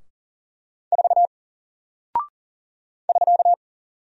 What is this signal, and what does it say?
Morse code, heard as a keyed tone, about 40 wpm, 700 Hz. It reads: 4 <SK>
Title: Single Character (Letter:Number:Punctuation:Prosign) 40wpm